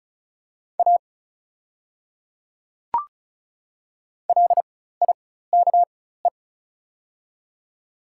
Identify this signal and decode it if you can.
Morse code, heard as a keyed tone, about 36 wpm, spaced as for 16 wpm, 690 Hz. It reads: A LIKE
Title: 4x Character Spacing - Top 100 Words - 35wpm